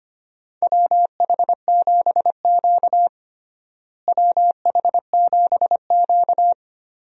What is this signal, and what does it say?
Morse code, heard as a keyed tone, about 25 wpm, 700 Hz. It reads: WH7Q WH7Q
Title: USA Call Signs - 4 Characters - Repeated 3x - 25wpm